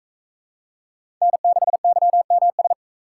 Morse code, heard as a keyed tone, about 41 wpm, 700 Hz. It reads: N6YMS